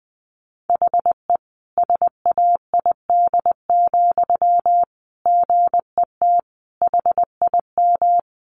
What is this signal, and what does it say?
Morse code, heard as a keyed tone, about 20 wpm, 700 Hz. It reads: HE SAID, GET HIM
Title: Sentences from Top 400 Words 20wpm